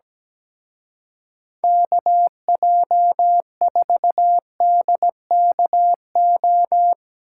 Morse code, heard as a keyed tone, about 18 wpm, 700 Hz. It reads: KJ4DKO
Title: Speed Racing - USA 2x3 Call Signs 17wpm